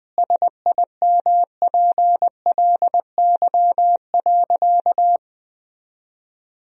Morse code, heard as a keyed tone, about 20 wpm, 700 Hz. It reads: SIMPLY.
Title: Sentences from Top 1000 Words - Review Part 4 - 20wpm